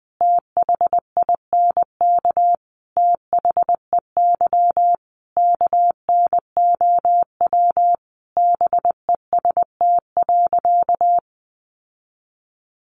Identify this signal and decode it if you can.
Morse code, heard as a keyed tone, about 20 wpm, 700 Hz. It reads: THINK THEY KNOW BEST.
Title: Sentences from Top 600 Words 20wpm